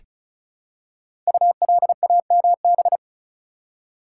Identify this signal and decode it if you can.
Morse code, heard as a keyed tone, about 36 wpm, 700 Hz. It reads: ULAMB
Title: Code Groups - 5 Letters - 35wpm